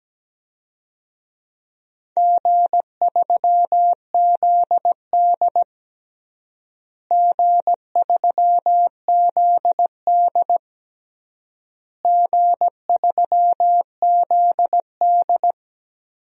Morse code, heard as a keyed tone, about 17 wpm, 700 Hz. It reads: G3ZD G3ZD G3ZD
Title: DXCC-100 Call Signs - 4 Characters - Repeated 3x - 17wpm